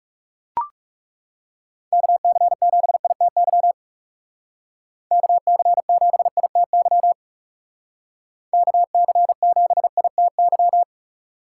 Morse code, heard as a keyed tone, about 40 wpm, 700 Hz. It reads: KC7ITY KC7ITY KC7ITY